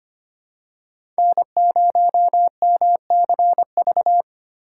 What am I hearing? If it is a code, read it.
Morse code, heard as a keyed tone, about 25 wpm, 700 Hz. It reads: N0MCV
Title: USA Call Signs - 5 Characters - 8x Character Spacing - 25wpm